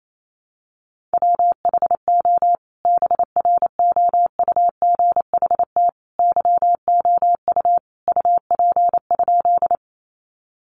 Morse code, heard as a keyed tone, about 28 wpm, 700 Hz. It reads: WHO BROUGHT YOU UP?
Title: Sentences from Top 500 Words 28wpm